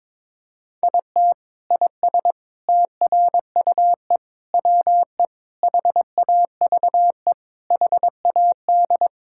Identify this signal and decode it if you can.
Morse code, heard as a keyed tone, about 22 wpm, 700 Hz. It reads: IT IS TRUE WE HAVE HAD
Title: Sentences from Top 1000 Words - Review Part 5 - 22wpm